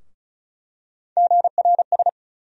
Morse code, heard as a keyed tone, about 36 wpm, 700 Hz. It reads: GRS
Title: USA Call Sign Postfixes - 3 Letters 35wpm